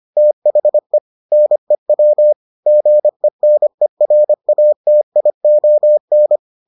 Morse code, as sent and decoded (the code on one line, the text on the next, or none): - .... / -. .-- / --. -. .-. .- - .. --- -.
THE NEW GENERATION